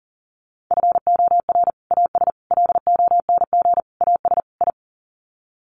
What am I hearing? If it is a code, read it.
Morse code, heard as a keyed tone, about 40 wpm, 700 Hz. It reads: FOR AS LONG AS I
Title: Sets of 5 Words Encore2 40wpm